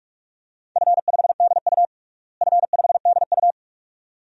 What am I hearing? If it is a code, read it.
Morse code, heard as a keyed tone, about 47 wpm, 700 Hz. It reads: F5DU F5DU
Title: DXCC-100 Call Signs - 4 Characters - Repeated 2x - 45wpm